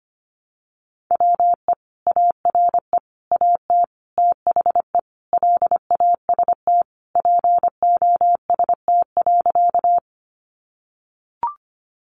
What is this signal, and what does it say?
Morse code, heard as a keyed tone, about 25 wpm, 700 Hz. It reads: WE ARE AT THE LAST POST.
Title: Sentences from Top 1000 Words 25wpm